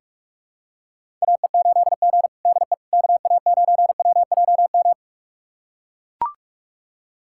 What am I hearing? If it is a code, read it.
Morse code, heard as a keyed tone, about 41 wpm, 700 Hz. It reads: AE8G DE KA9WJM
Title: USA Callsign DE Callsign 45wpm